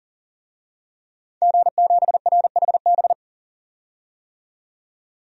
Morse code, heard as a keyed tone, about 38 wpm, 700 Hz. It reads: G7RHB
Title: DXCC-100 Call Signs - 5 Characters - 40wpm